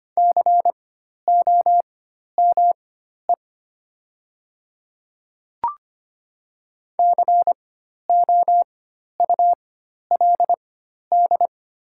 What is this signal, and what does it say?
Morse code, heard as a keyed tone, about 25 wpm, spaced as for 12 wpm, 700 Hz. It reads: COME COULD
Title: 4x Character Spacing - Top 200 Words - 25wpm